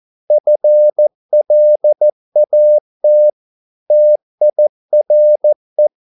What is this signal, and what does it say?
Morse code, heard as a keyed tone, about 14 wpm, 600 Hz. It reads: FLAT TIRE